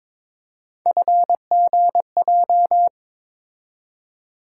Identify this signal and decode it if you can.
Morse code, heard as a keyed tone, about 22 wpm, 700 Hz. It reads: FGJ
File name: practice / 22wpm / Three Letter-Number Combinations - 22wpm